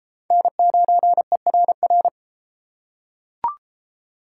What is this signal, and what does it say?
Morse code, heard as a keyed tone, about 31 wpm, 700 Hz. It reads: N9ERR